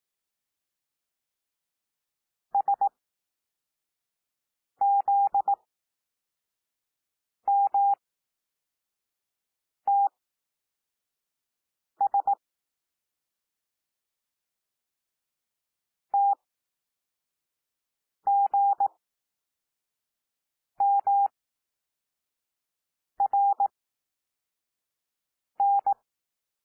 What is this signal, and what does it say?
Morse code, heard as a keyed tone, about 18 wpm, spaced as for 4 wpm, 800 Hz. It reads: SZMTS TGMRN
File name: cwmp3 / L2 SOUND LETTERS 18 wpm